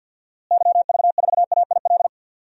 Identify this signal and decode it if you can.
Morse code, heard as a keyed tone, about 51 wpm, 700 Hz. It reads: YV4AIL
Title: DXCC-100 Call Signs - 6 Characters - Repeated 2x - 50wpm